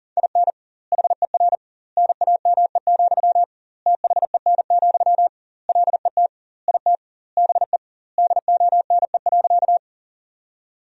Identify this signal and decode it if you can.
Morse code, heard as a keyed tone, about 40 wpm, 700 Hz. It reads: IN HER NAME, THEN, LET IT BE DONE.
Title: 2x Word-Spacing - Sentences from Top 500 Words - 40wpm